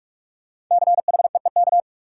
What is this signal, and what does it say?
Morse code, heard as a keyed tone, about 44 wpm, 700 Hz. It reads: CHEEK
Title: Five Letter Words - 45wpm